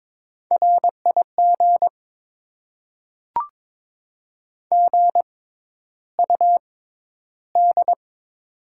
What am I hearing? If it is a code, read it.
Morse code, heard as a keyed tone, about 22 wpm, spaced as for 11 wpm, 700 Hz. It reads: RIG GUD
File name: Top 20 QSO Elements - 6x Character Spacing - 22wpm